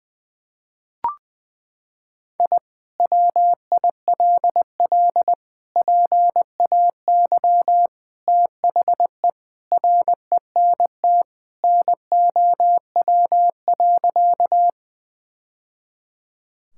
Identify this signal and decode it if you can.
Morse code, heard as a keyed tone, about 20 wpm, 700 Hz. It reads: I WILL PAY THE RENT NOW.
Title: Sentences from Top 1000 Words - Review Part 5 - 20wpm